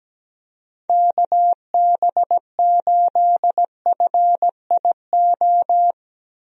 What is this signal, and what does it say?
Morse code, heard as a keyed tone, about 17 wpm, 700 Hz. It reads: KB8FIO
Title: USA Call Signs - 6 Characters - 17wpm